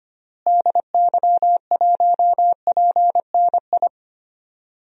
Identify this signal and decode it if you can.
Morse code, heard as a keyed tone, about 25 wpm, 700 Hz. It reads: DY1PNI
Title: DXCC-100 Call Signs - 6 Characters - Repeated 2x - 25wpm